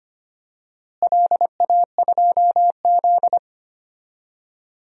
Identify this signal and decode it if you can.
Morse code, heard as a keyed tone, about 25 wpm, 700 Hz. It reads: LA2Z
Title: DXCC-100 Call Signs - 4 Characters - Repeated 3x - 25wpm